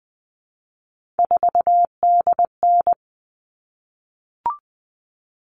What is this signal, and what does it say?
Morse code, heard as a keyed tone, about 20 wpm, 700 Hz. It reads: 4DN
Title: All Cut Numbers - 3-Digit 20wpm